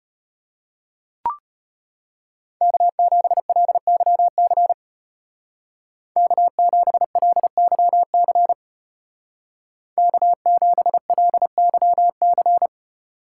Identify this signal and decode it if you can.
Morse code, heard as a keyed tone, about 31 wpm, 700 Hz. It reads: K7LYC K7LYC K7LYC
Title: Speed Racing - USA 1x3 Call Signs 25wpm